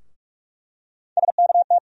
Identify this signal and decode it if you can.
Morse code, heard as a keyed tone, about 44 wpm, 700 Hz. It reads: SKT
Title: USA Call Sign Postfixes - 3 Letters 45wpm